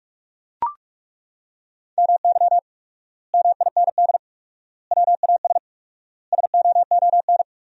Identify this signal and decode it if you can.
Morse code, heard as a keyed tone, about 41 wpm, 700 Hz. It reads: MY MIND WAS SOON
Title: 4x Word-Spacing - Sentences from Top 500 Words - 45wpm